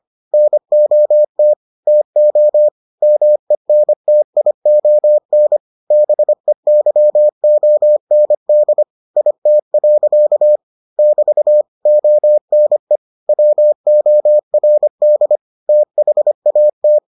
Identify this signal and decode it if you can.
Morse code, heard as a keyed tone, about 25 wpm, 600 Hz. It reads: NOT TO MENTION BEYOND IT. <BT> ONE WORD THAT